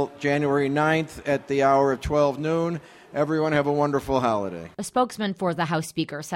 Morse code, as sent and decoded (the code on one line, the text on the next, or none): none